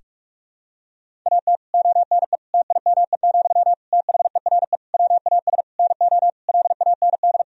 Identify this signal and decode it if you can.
Morse code, heard as a keyed tone, about 46 wpm, 700 Hz. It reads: AT ONE TIME, THERE WAS NO LAND
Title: Sentences from Top 1000 Words - Review Part 1 - 45wpm